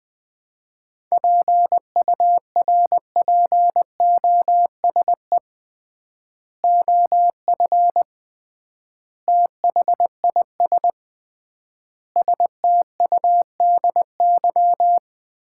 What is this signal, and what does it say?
Morse code, heard as a keyed tone, about 20 wpm, 700 Hz. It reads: PURPOSE OF THIS STUDY
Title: 3x Word-Spacing - Sets of 5 Words - 20wpm